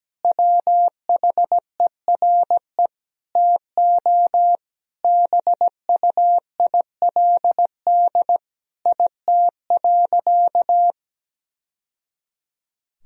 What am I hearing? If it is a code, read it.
Morse code, heard as a keyed tone, about 17 wpm, 700 Hz. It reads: WHERE TO BUILD IT.
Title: Sentences from Top 1000 Words - Review Part 1 - 17wpm